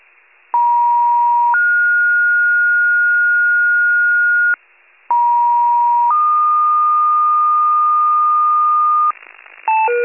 Two-tone page, heard as a one-second tone, then a three-second tone, about 950 then 1430 Hz, about 950 then 1220 Hz, with noise. It is heard from 0.5 to 9.1 s.
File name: wcalert